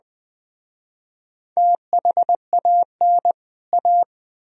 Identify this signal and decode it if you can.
Morse code, heard as a keyed tone, about 20 wpm, 700 Hz. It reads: THAN A
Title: Speed-Racing - Sets of 2 Words 17wpm